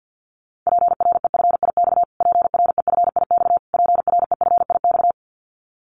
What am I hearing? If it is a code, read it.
Morse code, heard as a keyed tone, about 50 wpm, 700 Hz. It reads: PREFIX PREFIX PREFIX
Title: Callsign Prefixes for Top 25 DXCC - Repeated 3x - 50wpm